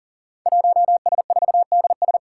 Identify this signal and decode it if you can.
Morse code, heard as a keyed tone, about 40 wpm, 700 Hz. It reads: 1S4DS
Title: Five Challenging Letter-Number Combinations 40wpm